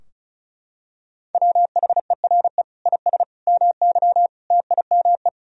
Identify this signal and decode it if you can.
Morse code, heard as a keyed tone, about 34 wpm, 700 Hz. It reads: WHERE IS MY TIME
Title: Sentences from Top 1000 Words - Review Part 2 - 35wpm